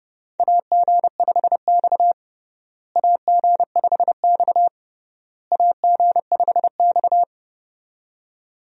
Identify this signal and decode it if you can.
Morse code, heard as a keyed tone, about 32 wpm, 700 Hz. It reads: AG5X AG5X AG5X
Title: USA Call Signs - 4 Characters - Repeated 3x - 30wpm